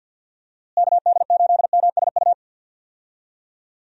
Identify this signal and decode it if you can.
Morse code, heard as a keyed tone, about 52 wpm, 700 Hz. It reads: KD8MSU
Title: USA Call Signs - 6 Characters - Repeated 3x - 50wpm